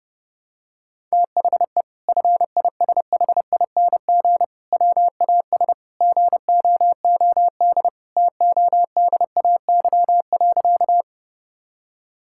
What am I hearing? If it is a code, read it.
Morse code, heard as a keyed tone, about 30 wpm, 700 Hz. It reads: THE FISHING WAS GOOD TODAY.
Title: Sentences from Top 1000 Words - Review Part 5 - 30wpm